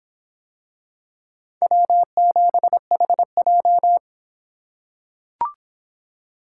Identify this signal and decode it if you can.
Morse code, heard as a keyed tone, about 25 wpm, 700 Hz. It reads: W7HJ